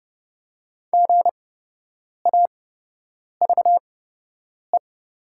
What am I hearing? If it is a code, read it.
Morse code, heard as a keyed tone, about 30 wpm, spaced as for 8 wpm, 700 Hz. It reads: GAVE
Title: 8x Character Spacing - Top 500 Words - 30wpm